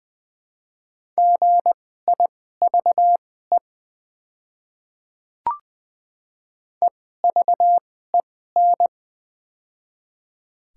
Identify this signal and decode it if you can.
Morse code, heard as a keyed tone, about 20 wpm, spaced as for 14 wpm, 700 Hz. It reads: GIVE EVEN
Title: 2x Character Spacing - Top 200 Words - 20wpm